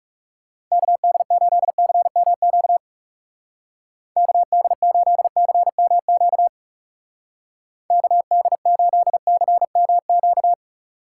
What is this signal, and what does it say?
Morse code, heard as a keyed tone, about 40 wpm, 700 Hz. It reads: KD8CMQ KD8CMQ KD8CMQ